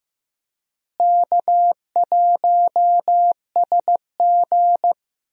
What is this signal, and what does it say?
Morse code, heard as a keyed tone, about 15 wpm, 700 Hz. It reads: K1SG
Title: USA Call Signs - 4 Characters - Repeated 2x - 15wpm